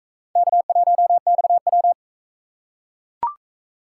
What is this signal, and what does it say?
Morse code, heard as a keyed tone, about 44 wpm, 700 Hz. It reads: K1XW